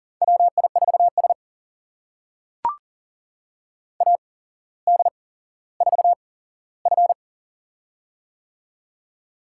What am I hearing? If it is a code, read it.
Morse code, heard as a keyed tone, about 39 wpm, spaced as for 17 wpm, 700 Hz. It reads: WI4S AD4F
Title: USA Call Signs - 4 Characters - 8x Character Spacing - 40wpm